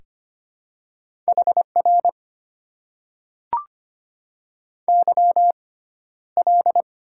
Morse code, heard as a keyed tone, about 25 wpm, spaced as for 13 wpm, 700 Hz. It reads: HR YL
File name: Top 90 QSO Elements - 6x Character Spacing - 25wpm